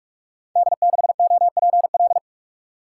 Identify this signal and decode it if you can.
Morse code, heard as a keyed tone, about 44 wpm, 700 Hz. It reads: D6OPL
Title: Code Groups - 5 Letter:Numbers - 45wpm